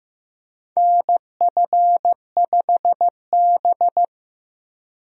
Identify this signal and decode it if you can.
Morse code, heard as a keyed tone, about 15 wpm, 700 Hz. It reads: NF5B
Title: USA Call Signs - 4 Characters - Repeated 2x - 15wpm